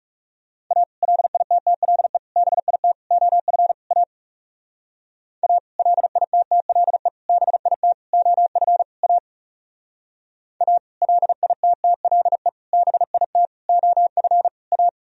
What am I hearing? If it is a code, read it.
Morse code, heard as a keyed tone, about 37 wpm, 700 Hz. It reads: A LITTLE BIT OF A A LITTLE BIT OF A A LITTLE BIT OF A
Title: Speed-Racing - Sets of 5 Words 30wpm